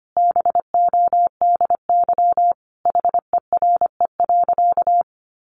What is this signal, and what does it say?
Morse code, heard as a keyed tone, about 25 wpm, 700 Hz. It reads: BODY HERE.
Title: Sentences from Top 1100 Words 25wpm